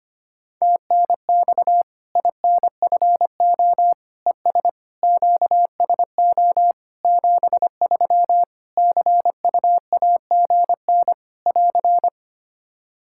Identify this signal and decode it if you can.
Morse code, heard as a keyed tone, about 25 wpm, 700 Hz. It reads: TNX INFO ES QSO 73 CUAGN <AR>
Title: Practice QSO - Exchange 4 - Ending - 25wpm